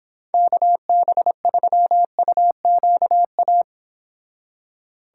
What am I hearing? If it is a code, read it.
Morse code, heard as a keyed tone, about 31 wpm, 700 Hz. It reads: KB3UQA